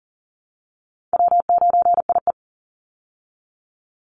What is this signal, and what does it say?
Morse code, heard as a keyed tone, about 40 wpm, 700 Hz. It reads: W9IE
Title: Easy Call Signs Repeated 2x - 40wpm